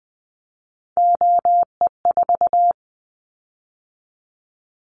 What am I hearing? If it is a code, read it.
Morse code, heard as a keyed tone, about 20 wpm, 700 Hz. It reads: OE4
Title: Callsign Prefixes for Top 100 DXCC - 20wpm